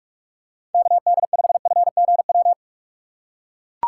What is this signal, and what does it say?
Morse code, heard as a keyed tone, about 43 wpm, 700 Hz. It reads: KD5FGW
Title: USA Call Signs - 6 Characters - 3x Character Spacing - 45wpm